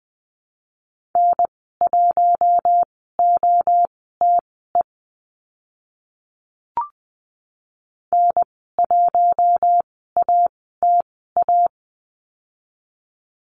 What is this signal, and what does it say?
Morse code, heard as a keyed tone, about 20 wpm, spaced as for 14 wpm, 700 Hz. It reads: N1OTE N1ATA
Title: 042 Lesson 8g - Number 1 - Call Signs 20wpm